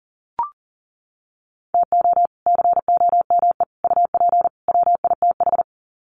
Morse code, heard as a keyed tone, about 40 wpm, 700 Hz. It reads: TO COME UP WITH
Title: Sets of 4 Words 40wpm